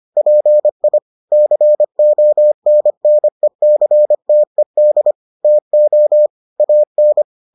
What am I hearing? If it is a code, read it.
Morse code, heard as a keyed tone, about 25 wpm, 600 Hz. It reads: PI CONNECTED TO AN